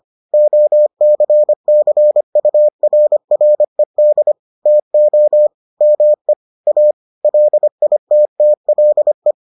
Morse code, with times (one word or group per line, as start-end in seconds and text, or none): 0.3-4.3 OCCURRED
4.7-5.5 TO
5.8-6.3 ME
6.7-6.9 A
7.2-9.3 LITTLE